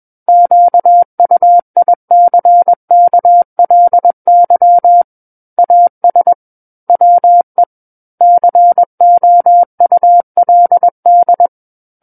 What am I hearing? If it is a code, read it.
Morse code, heard as a keyed tone, about 21 wpm, 700 Hz. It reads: QUICKLY AS WE COULD
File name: Book One / 12-Chapter 12 - 21 WPM